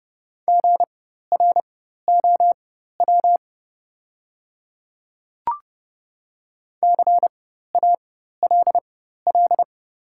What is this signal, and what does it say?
Morse code, heard as a keyed tone, about 30 wpm, spaced as for 14 wpm, 700 Hz. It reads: GROW CALL
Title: 4x Character Spacing - Top 300 Words - 30wpm